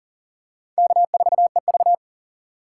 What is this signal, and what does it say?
Morse code, heard as a keyed tone, about 40 wpm, 700 Hz. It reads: K4EV
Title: USA Call Signs - 4 Characters - 4x Character Spacing - 40wpm